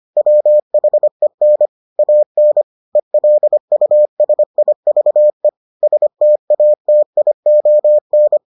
Morse code, heard as a keyed tone, about 25 wpm, 600 Hz. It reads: WHEN AN ELUSIVE STATION